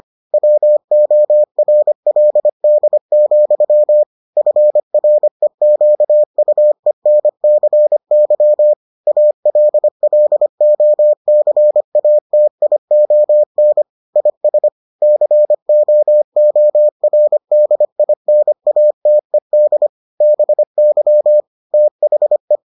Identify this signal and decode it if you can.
Morse code, heard as a keyed tone, about 25 wpm, 600 Hz. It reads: WORLD, FREQUENCY ALLOCATION IS COORDINATED BY THE